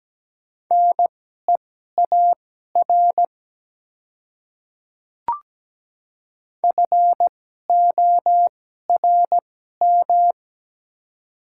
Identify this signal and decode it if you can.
Morse code, heard as a keyed tone, about 17 wpm, spaced as for 12 wpm, 700 Hz. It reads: NEAR FORM
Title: 2x Character Spacing - Top 200 Words - 17wpm